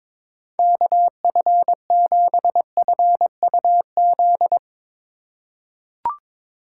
Morse code, heard as a keyed tone, about 22 wpm, 700 Hz. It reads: KF7FUZ
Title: USA Call Signs - 6 Characters - 4x Character Spacing - 22wpm